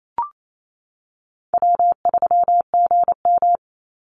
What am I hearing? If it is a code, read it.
Morse code, heard as a keyed tone, about 28 wpm, 700 Hz. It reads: W3GM
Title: USA Call Signs 28wpm